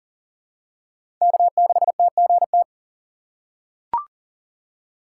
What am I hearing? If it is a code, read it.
Morse code, heard as a keyed tone, about 40 wpm, 700 Hz. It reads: K6TGT
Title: USA Call Signs - 5 Characters - 2x Character Spacing - 40wpm